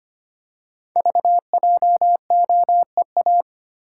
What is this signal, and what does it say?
Morse code, heard as a keyed tone, about 25 wpm, 690 Hz. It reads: VJOEA
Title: Code Groups - 5 Letters - 25wpm